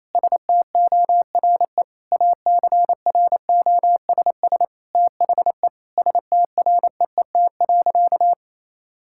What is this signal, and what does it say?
Morse code, heard as a keyed tone, about 28 wpm, 700 Hz. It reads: STORE ACROSS THE STREET.
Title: Sentences from Top 1000 Words - Review Part 5 - 28wpm